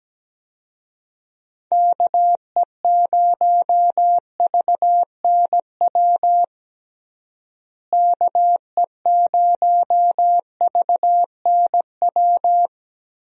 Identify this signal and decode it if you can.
Morse code, heard as a keyed tone, about 17 wpm, 700 Hz. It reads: KE0VNW KE0VNW
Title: USA Call Signs - 6 Characters - Repeated 3x - 17wpm